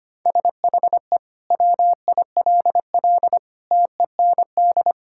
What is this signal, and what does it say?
Morse code, heard as a keyed tone, about 25 wpm, 700 Hz. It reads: SHE WILL TEND
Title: Sentences from Top 1000 Words - Review Part 5 - 25wpm